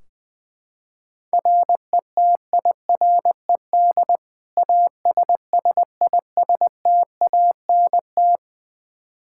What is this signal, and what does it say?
Morse code, heard as a keyed tone, about 20 wpm, 700 Hz. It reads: RETIRED ASSISTANT
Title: Practice QSO - Exchange 3 - More Info - 20wpm